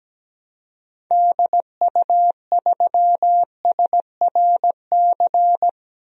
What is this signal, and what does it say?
Morse code, heard as a keyed tone, about 17 wpm, 700 Hz. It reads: DU3SRC
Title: DXCC-100 Call Signs - 6 Characters - 17wpm